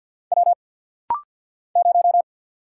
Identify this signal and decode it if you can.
Morse code, heard as a keyed tone, about 49 wpm, 700 Hz. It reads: W 0
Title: Single Letter-Number - Rapid-Fire 50wpm